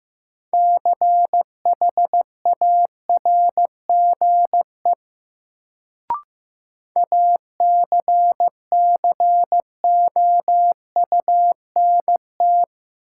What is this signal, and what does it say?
Morse code, heard as a keyed tone, about 15 wpm, 700 Hz. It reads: CHARGE ACCOUNT
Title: Sentences from Top 900 Words - New Word Drill - 15wpm